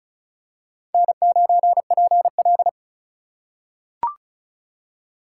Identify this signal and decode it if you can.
Morse code, heard as a keyed tone, about 36 wpm, 700 Hz. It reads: N9PL
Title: USA Call Signs - 4 Characters - 2x Character Spacing - 35wpm